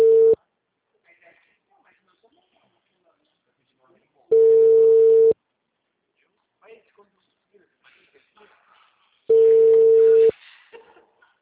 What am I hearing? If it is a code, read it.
Ringback tone (China, 450 Hz), with speech.
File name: out- -2019- - - .9744